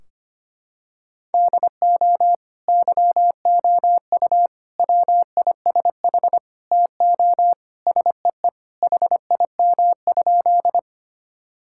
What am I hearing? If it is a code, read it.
Morse code, heard as a keyed tone, about 25 wpm, 700 Hz. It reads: DO YOU WISH TO SEE HIM?
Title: Sentences from Top 1600 Words 25wpm